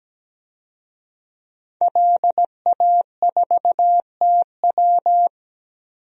Morse code, heard as a keyed tone, about 17 wpm, 700 Hz. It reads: LA4TW
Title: DXCC-100 Call Signs - 5 Characters - 17wpm